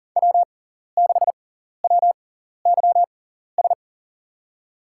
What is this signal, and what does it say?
Morse code, heard as a keyed tone, about 42 wpm, spaced as for 14 wpm, 700 Hz. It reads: W6WYS
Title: USA Call Signs - 5 Characters - 6x Character Spacing - 40wpm